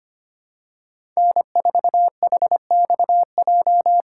Morse code, heard as a keyed tone, about 25 wpm, 700 Hz. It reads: N4HXJ
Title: USA Call Signs - 5 Characters - 6x Character Spacing - 25wpm